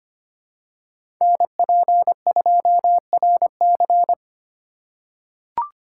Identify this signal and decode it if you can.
Morse code, heard as a keyed tone, about 25 wpm, 700 Hz. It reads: NP2RC
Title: USA Call Signs - 5 Characters - 3x Character Spacing - 25wpm